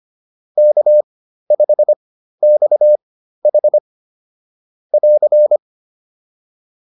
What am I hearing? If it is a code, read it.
Morse code, heard as a keyed tone, about 25 wpm, spaced as for 13 wpm, 600 Hz. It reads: K5XH <AR>